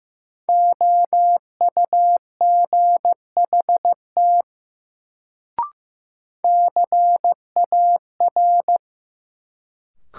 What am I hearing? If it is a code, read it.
Morse code, heard as a keyed tone, about 15 wpm, 700 Hz. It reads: OUGHT CAR
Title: Sets of 5 Words Encore - New Word Drill - 15wpm